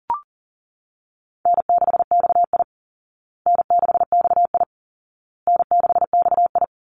Morse code, heard as a keyed tone, about 40 wpm, 700 Hz. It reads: N6XI N6XI N6XI
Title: Easy Call Signs Repeated 3x - 40wpm